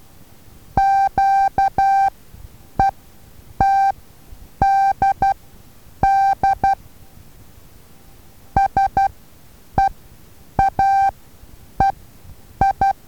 Morse code, heard as a keyed tone, about 12 wpm, spaced as for 8 wpm, 780 Hz. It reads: QETDD SEAEI
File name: Taviro HA6VA A